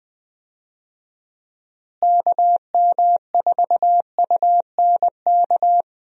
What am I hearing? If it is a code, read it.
Morse code, heard as a keyed tone, about 20 wpm, 700 Hz. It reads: KM4UNK
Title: USA Call Signs - 6 Characters - Repeated 2x - 20wpm